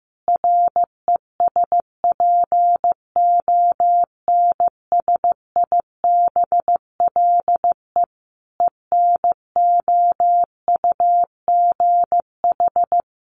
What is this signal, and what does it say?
Morse code, heard as a keyed tone, about 15 wpm, 700 Hz. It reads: RESPONSIBLE ENOUGH
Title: Additional Easy Phrases 15wpm